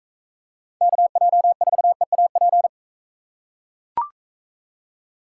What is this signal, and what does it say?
Morse code, heard as a keyed tone, about 41 wpm, 700 Hz. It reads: KJ4EAP